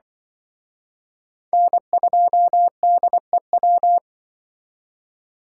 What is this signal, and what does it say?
Morse code, heard as a keyed tone, about 25 wpm, 700 Hz. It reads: N2DEW
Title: Speed Racing - USA Call Signs 20wpm